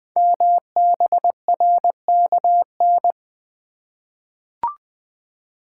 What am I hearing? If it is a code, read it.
Morse code, heard as a keyed tone, about 20 wpm, 700 Hz. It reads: MBRKN
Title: Code Groups - 5 Letters - 20wpm